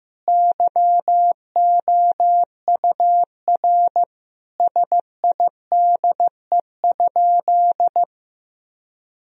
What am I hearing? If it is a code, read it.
Morse code, heard as a keyed tone, about 16 wpm, 700 Hz. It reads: YOUR SIDE?
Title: Sentences from Top 1000 Words - Review Part 1 - 15wpm